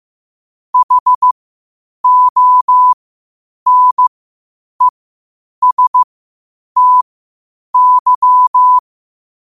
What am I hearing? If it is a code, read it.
Morse code, heard as a keyed tone, about 15 wpm, spaced as for 8 wpm, 1000 Hz. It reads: HONESTY
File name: qotdfiles / 8wpm